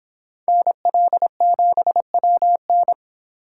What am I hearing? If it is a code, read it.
Morse code, heard as a keyed tone, about 26 wpm, 700 Hz. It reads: NL7WN